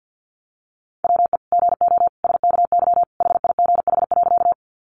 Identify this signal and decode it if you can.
Morse code, heard as a keyed tone, about 50 wpm, 700 Hz. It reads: WE GO SKY HIGH.
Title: Sentences from Top 1200 Words 50wpm